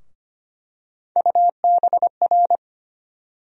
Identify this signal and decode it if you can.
Morse code, heard as a keyed tone, about 25 wpm, 700 Hz. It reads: UBR
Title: USA Call Sign Postfixes - 3 Letters 25wpm